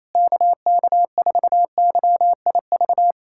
Morse code, heard as a keyed tone, about 28 wpm, 700 Hz. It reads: KK4YIV
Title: USA Callsign DE Callsign - 6 Characters - 28wpm